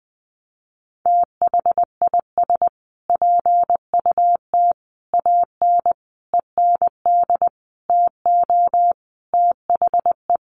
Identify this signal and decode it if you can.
Morse code, heard as a keyed tone, about 20 wpm, 700 Hz. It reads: THIS PUT AN END TO THE
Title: Sentences from Top 1100 Words 20wpm